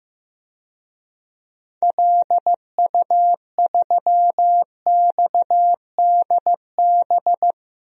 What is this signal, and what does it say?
Morse code, heard as a keyed tone, about 15 wpm, 700 Hz. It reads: LU3XDB
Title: DXCC-100 Call Signs - 6 Characters - Repeated 3x - 15wpm